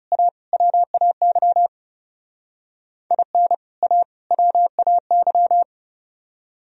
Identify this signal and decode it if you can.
Morse code, heard as a keyed tone, about 34 wpm, 700 Hz. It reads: A WAY IN A WAY
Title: Speed-Racing - Sets of 3 Words 30wpm